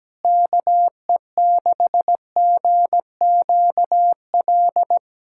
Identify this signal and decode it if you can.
Morse code, heard as a keyed tone, about 17 wpm, 700 Hz. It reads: KE6GQL
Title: USA Callsign DE Callsign 17wpm